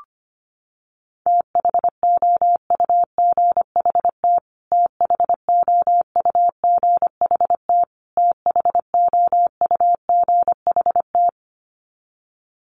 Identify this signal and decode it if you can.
Morse code, heard as a keyed tone, about 25 wpm, 700 Hz. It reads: THOUGHT THOUGHT THOUGHT
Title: Top 500 Words Repeated 25wpm